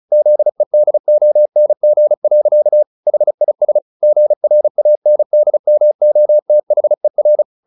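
Morse code, heard as a keyed tone, about 36 wpm, 600 Hz. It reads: ZEDONG. HIS GRANDMOTHER
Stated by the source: Morse practice at 35 wpm from CW med Gnister, sending Ken Follett's novel Never in English.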